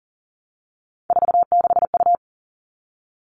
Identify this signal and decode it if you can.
Morse code, heard as a keyed tone, about 40 wpm, 700 Hz. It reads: 46U
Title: All Cut Numbers - 3-Digit 40wpm